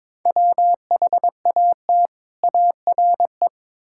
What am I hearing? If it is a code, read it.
Morse code, heard as a keyed tone, about 23 wpm, 700 Hz. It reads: WHAT ARE
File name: practice / 22wpm / Build A Phrase - 5-Word Phrases - 22wpm